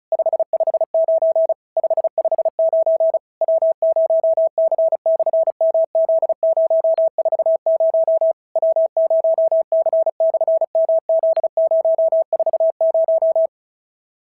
Morse code, heard as a keyed tone, about 36 wpm, 650 Hz. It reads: 559 559 W0C/MZ040 W0C/MZ040
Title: SOTA QSO Practice 35wpm